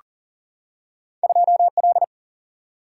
Morse code, heard as a keyed tone, about 40 wpm, 700 Hz. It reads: 2L
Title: Single-Letter Single-Number 40wpm